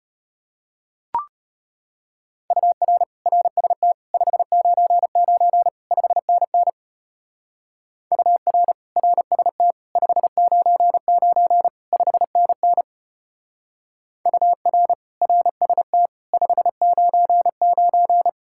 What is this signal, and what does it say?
Morse code, heard as a keyed tone, about 31 wpm, 700 Hz. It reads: UR RST 599 5NN UR RST 599 5NN UR RST 599
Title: Speed-Racing Common QSO Phrases - 25wpm